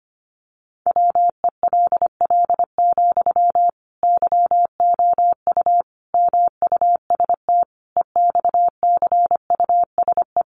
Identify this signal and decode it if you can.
Morse code, heard as a keyed tone, about 25 wpm, 700 Hz. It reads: WELL, YOU MUST EXCUSE
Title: Sentences from Top 700 Words 25wpm